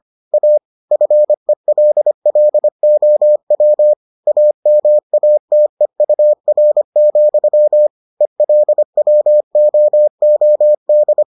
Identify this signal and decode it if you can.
Morse code, heard as a keyed tone, about 25 wpm, 600 Hz. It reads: A FELLOW AMATEUR, ELWOOD